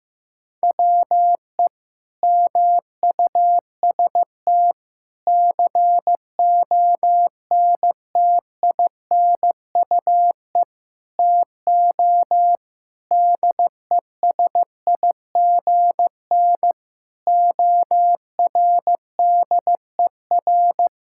Morse code, heard as a keyed tone, about 15 wpm, 700 Hz. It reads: WE MUST CONTINUE TO DESIGN ORDER
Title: Sentences from Top 1000 Words - Review Part 6 - 15wpm